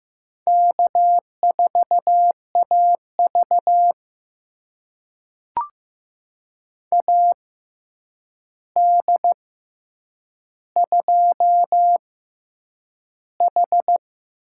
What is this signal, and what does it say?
Morse code, heard as a keyed tone, about 15 wpm, spaced as for 8 wpm, 700 Hz. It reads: K4AV AD2H
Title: USA Call Signs - 4 Characters - 6x Character Spacing - 15wpm